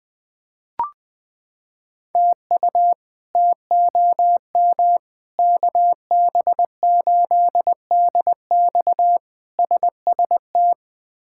Morse code, heard as a keyed tone, about 20 wpm, 700 Hz. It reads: TU TOM KB8DX SST
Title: K1USN SST QSO Practice - 20wpm